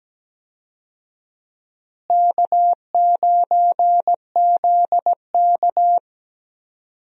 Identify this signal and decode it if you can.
Morse code, heard as a keyed tone, about 17 wpm, 700 Hz. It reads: K9ZK
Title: USA Call Signs - 4 Characters - Repeated 2x - 17wpm